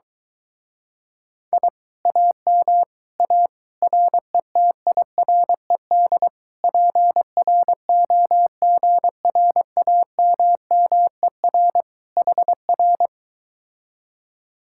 Morse code, heard as a keyed tone, about 23 wpm, 700 Hz. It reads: I AM A RETIRED PROGRAMMER HR